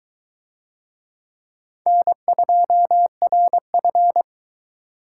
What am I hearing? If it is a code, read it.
Morse code, heard as a keyed tone, about 21 wpm, 700 Hz. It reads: N2RF